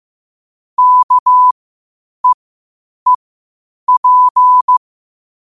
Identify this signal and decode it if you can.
Morse code, heard as a keyed tone, about 15 wpm, spaced as for 8 wpm, 1000 Hz. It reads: KEEP